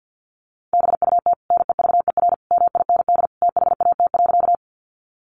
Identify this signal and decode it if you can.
Morse code, heard as a keyed tone, about 50 wpm, 700 Hz. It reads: BUT NEVER MIND THAT.
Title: Sentences from Top 500 Words 50wpm